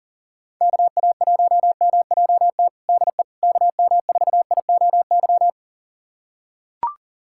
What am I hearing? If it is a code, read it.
Morse code, heard as a keyed tone, about 37 wpm, 700 Hz. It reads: KA1MJT DE KM4IOY
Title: USA Callsign DE Callsign - 6 Characters - 40wpm